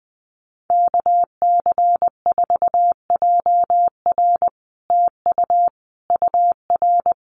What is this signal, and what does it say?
Morse code, heard as a keyed tone, about 20 wpm, 700 Hz. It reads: KC4JR TU UR
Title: Common QSO Phrases Encore 20wpm